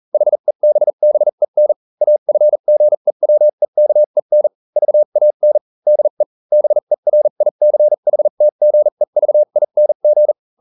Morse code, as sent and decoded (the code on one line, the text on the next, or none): .... -... -... -. / .- ..-. --. .-- . -.- . -. / ...- .- -. / -.. / -... .-. .. -.-. .... - --. ...- .. -. --.
HEBBEN AFGEWEKEN VAN DE BERICHTGEVING